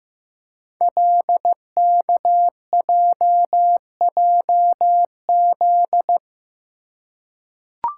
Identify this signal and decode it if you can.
Morse code, heard as a keyed tone, about 15 wpm, 700 Hz. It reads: LKJJZ
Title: Code Groups - 5 Letters - 15wpm